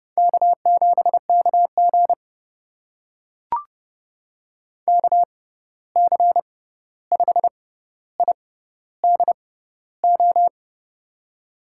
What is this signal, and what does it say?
Morse code, heard as a keyed tone, about 30 wpm, spaced as for 14 wpm, 700 Hz. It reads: K7KG KC5IDO
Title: USA Call Signs - 6x Character Spacing - 30wpm